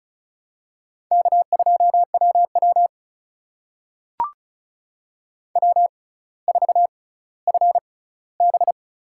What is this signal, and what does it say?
Morse code, heard as a keyed tone, about 36 wpm, spaced as for 18 wpm, 700 Hz. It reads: K2WW W4FB
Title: USA Call Signs - 4 Characters - 6x Character Spacing - 35wpm